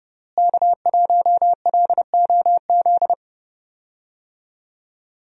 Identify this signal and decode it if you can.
Morse code, heard as a keyed tone, about 32 wpm, 700 Hz. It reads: K1LOZ
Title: USA Call Signs - 5 Characters - Repeated 3x - 30wpm